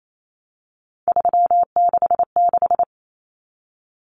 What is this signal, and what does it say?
Morse code, heard as a keyed tone, about 28 wpm, 700 Hz. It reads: 366
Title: Numbers - 3-Digit 28wpm